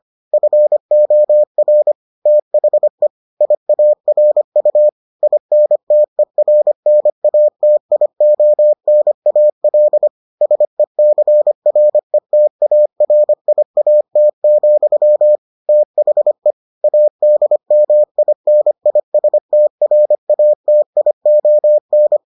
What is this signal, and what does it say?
Morse code, heard as a keyed tone, about 25 wpm, 600 Hz. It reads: FOR THE IARU INTERNATIONAL SECRETARIAT, THE ADMINISTRATION